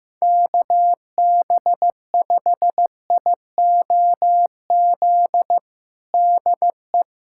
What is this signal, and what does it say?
Morse code, heard as a keyed tone, about 16 wpm, 700 Hz. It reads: KB5IOZ DE
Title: USA Callsign DE Callsign 15wpm